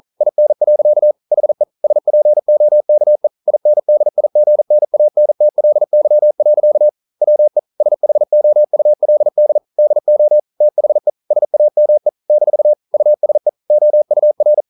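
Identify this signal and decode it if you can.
Morse code, heard as a keyed tone, about 40 wpm, 600 Hz. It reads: IN. HE SPOKE INDIGNANTLY. WE SHOULD DO THE SAME - USE OUR
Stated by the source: Morse code practice at 41 wpm, from CW med Gnister.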